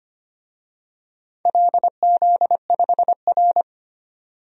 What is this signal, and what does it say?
Morse code, heard as a keyed tone, about 25 wpm, 700 Hz. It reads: LZ5R
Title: DXCC-100 Call Signs - 4 Characters - 25wpm